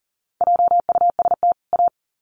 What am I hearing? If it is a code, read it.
Morse code, heard as a keyed tone, about 40 wpm, 700 Hz. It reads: JUST A
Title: Sets of 2 Words 40wpm